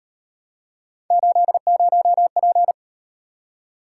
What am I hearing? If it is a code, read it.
Morse code, heard as a keyed tone, about 36 wpm, 700 Hz. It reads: 80P